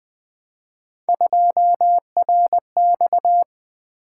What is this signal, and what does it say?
Morse code, heard as a keyed tone, about 20 wpm, 700 Hz. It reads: 2RX
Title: Three Letter-Number Combinations - 20wpm